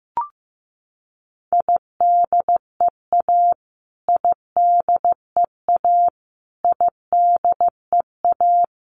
Morse code, heard as a keyed tone, about 15 wpm, 700 Hz. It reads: IDEA IDEA IDEA
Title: Top 500 Words Repeated 15wpm